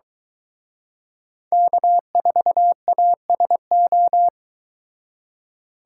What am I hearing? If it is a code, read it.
Morse code, heard as a keyed tone, about 23 wpm, 700 Hz. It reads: K4ASO